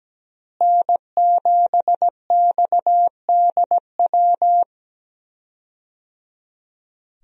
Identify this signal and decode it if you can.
Morse code, heard as a keyed tone, about 17 wpm, 700 Hz. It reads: N7XDW
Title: USA Call Signs - 5 Characters - Repeated 3x - 17wpm